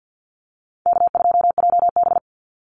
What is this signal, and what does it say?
Morse code, heard as a keyed tone, about 52 wpm, 700 Hz. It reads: K2JB
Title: NA SOTA Activator Callsigns 50wpm